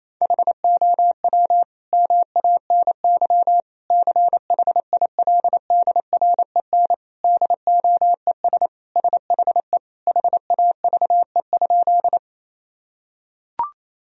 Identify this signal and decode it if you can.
Morse code, heard as a keyed tone, about 28 wpm, 700 Hz. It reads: HOW MANY CHILDREN DOES SHE HAVE?
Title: Sentences from Top 1000 Words - Review Part 1 - 28wpm